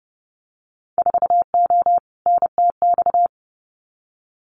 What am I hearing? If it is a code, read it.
Morse code, heard as a keyed tone, about 30 wpm, 700 Hz. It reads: 4O NTX
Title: Winter Field Day Exchanges 30wpm